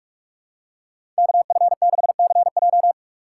Morse code, heard as a keyed tone, about 46 wpm, 700 Hz. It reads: KF6CJ
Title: USA Call Signs - 5 Characters - 3x Character Spacing - 45wpm